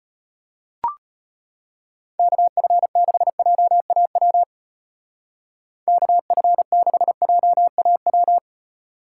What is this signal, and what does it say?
Morse code, heard as a keyed tone, about 35 wpm, 700 Hz. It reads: KF6JAW KF6JAW